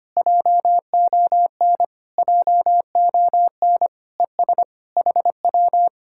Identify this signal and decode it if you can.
Morse code, heard as a keyed tone, about 24 wpm, 700 Hz. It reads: JON JON ES HW
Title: Speed-Racing Common QSO Phrases Encore - 25wpm